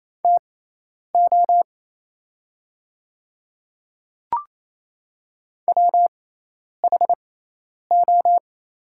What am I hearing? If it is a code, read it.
Morse code, heard as a keyed tone, about 27 wpm, spaced as for 10 wpm, 700 Hz. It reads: TO WHO
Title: 6x Character Spacing - Top 300 Words - 28wpm